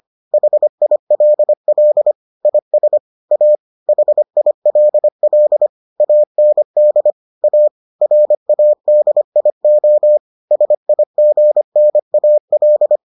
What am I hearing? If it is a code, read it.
Morse code, heard as a keyed tone, about 25 wpm, 600 Hz. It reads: HILL IS A HILL AND A RADIO SIGNAL